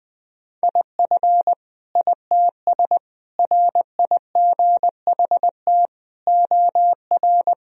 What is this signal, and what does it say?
Morse code, heard as a keyed tone, about 20 wpm, 700 Hz. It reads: IF ITS RIGHT OR
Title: Sentences from Top 1000 Words - Review Part 3 - 20wpm